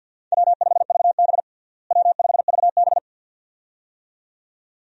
Morse code, heard as a keyed tone, about 52 wpm, 700 Hz. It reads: W5VB W5VB
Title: USA Call Signs - 4 Characters - Repeated 2x - 50wpm